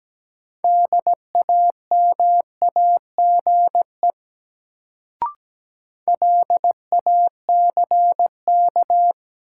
Morse code, heard as a keyed tone, about 17 wpm, 700 Hz. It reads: DAMAGE LACK
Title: Sentences from Top 1100 Words - New Word Drill Alt - 17wpm